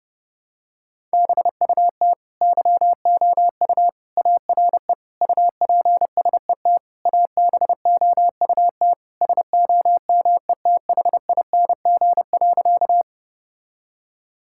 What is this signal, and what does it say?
Morse code, heard as a keyed tone, about 32 wpm, 700 Hz. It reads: BUT YOU ARE UPSET ABOUT SOMETHING.
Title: Sentences from Top 1000 Words - Review Part 5 - 30wpm